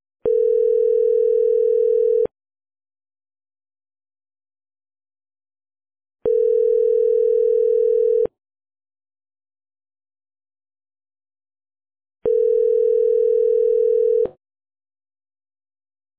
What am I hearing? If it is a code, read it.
Ringback tone (North America), 440 + 480 Hz.